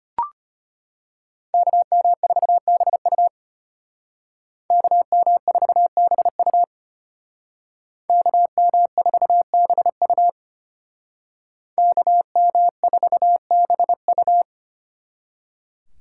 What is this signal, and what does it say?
Morse code, heard as a keyed tone, about 32 wpm, 700 Hz. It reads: KM4BU KM4BU KM4BU KM4BU